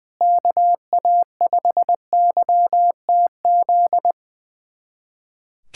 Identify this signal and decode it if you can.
Morse code, heard as a keyed tone, about 20 wpm, 700 Hz. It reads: KA5YTZ
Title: USA Callsign DE Callsign - 6 Characters - 20wpm